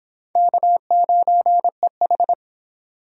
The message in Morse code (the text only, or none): K9EH